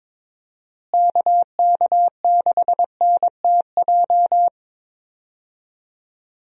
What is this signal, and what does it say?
Morse code, heard as a keyed tone, about 22 wpm, 700 Hz. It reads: KK6NTJ
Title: USA Call Signs - 6 Characters - Repeated 3x - 22wpm